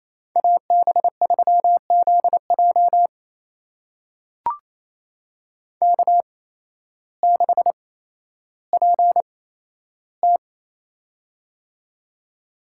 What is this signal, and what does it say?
Morse code, heard as a keyed tone, about 28 wpm, spaced as for 13 wpm, 700 Hz. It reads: AB3ZJ K6PT
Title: USA Call Signs - 8x Character Spacing - 28wpm